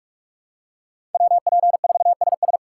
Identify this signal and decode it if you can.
Morse code, heard as a keyed tone, about 46 wpm, 700 Hz. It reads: WP4SS
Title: USA Call Signs - 5 Characters - 2x Character Spacing - 45wpm